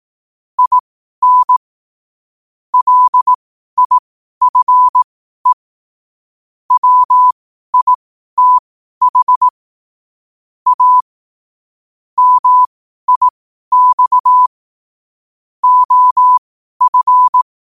Morse code, heard as a keyed tone, about 18 wpm, spaced as for 12 wpm, 1000 Hz. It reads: IN LIFE WITH A MIX OF
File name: qotdfiles / 12wpm